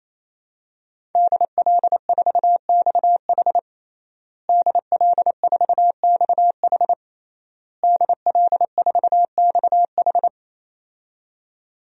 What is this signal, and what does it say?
Morse code, heard as a keyed tone, about 27 wpm, 700 Hz. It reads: DL4XH DL4XH DL4XH
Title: DXCC-100 Call Signs - 5 Characters - Repeated 3x - 28wpm